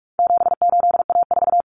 Morse code, heard as a keyed tone, about 44 wpm, 700 Hz. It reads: 78A4
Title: Common Cut Numbers - 4-Digit 45wpm